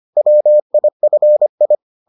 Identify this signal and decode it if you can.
Morse code, heard as a keyed tone, about 25 wpm, 600 Hz. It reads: WIFI